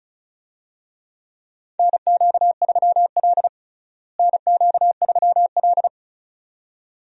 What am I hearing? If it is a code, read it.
Morse code, heard as a keyed tone, about 34 wpm, 700 Hz. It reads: NQ3L NQ3L
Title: USA Call Signs - 4 Characters - Repeated 2x - 35wpm